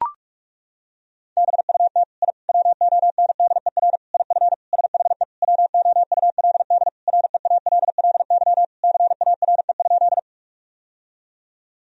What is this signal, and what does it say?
Morse code, heard as a keyed tone, about 42 wpm, 700 Hz. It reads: BUT I WONDER IF SHE WOULD REALLY CARE?
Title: Sentences from Top 1000 Words - Review Part 3 - 45wpm